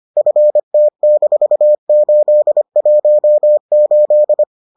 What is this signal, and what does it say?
Morse code, heard as a keyed tone, about 25 wpm, 600 Hz. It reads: FT-818